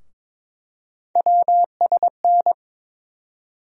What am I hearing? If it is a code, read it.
Morse code, heard as a keyed tone, about 22 wpm, 700 Hz. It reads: WSN
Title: USA Call Sign Postfixes - 3 Letters 22wpm